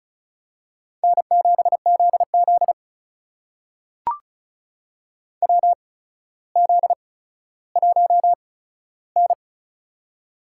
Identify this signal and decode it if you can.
Morse code, heard as a keyed tone, about 36 wpm, spaced as for 15 wpm, 700 Hz. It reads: N7ZZ WZ1N
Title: USA Call Signs - 4 Characters - 8x Character Spacing - 35wpm